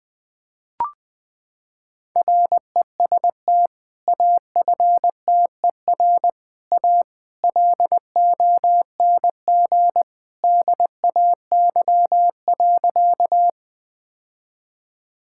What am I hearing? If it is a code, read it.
Morse code, heard as a keyed tone, about 21 wpm, 700 Hz. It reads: REST AFTER A LONG DAY.
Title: Sentences with 2-Word Phrases - 20wpm